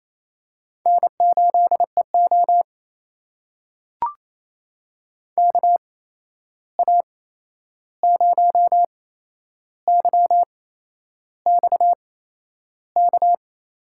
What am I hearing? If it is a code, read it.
Morse code, heard as a keyed tone, about 28 wpm, spaced as for 11 wpm, 700 Hz. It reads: N8EO KA0YXK
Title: USA Call Signs - 8x Character Spacing - 28wpm